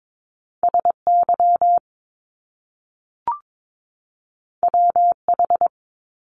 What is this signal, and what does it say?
Morse code, heard as a keyed tone, about 22 wpm, 700 Hz. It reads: SY WH
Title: Top Two Letter Combinations in Words 22wpm